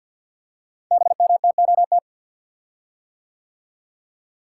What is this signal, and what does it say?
Morse code, heard as a keyed tone, about 48 wpm, 700 Hz. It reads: BMTOT
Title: Code Groups - 5 Letters - 50wpm